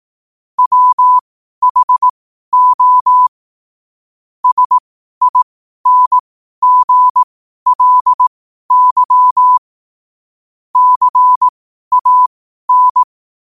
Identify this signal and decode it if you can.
Morse code, heard as a keyed tone, about 18 wpm, spaced as for 12 wpm, 1000 Hz. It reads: WHO SINGLY CAN
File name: qotdfiles / 12wpm